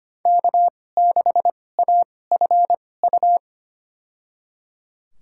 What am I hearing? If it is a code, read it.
Morse code, heard as a keyed tone, about 25 wpm, spaced as for 18 wpm, 700 Hz. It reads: K6AFU
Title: USA Call Signs - 2x Character Spacing - 25wpm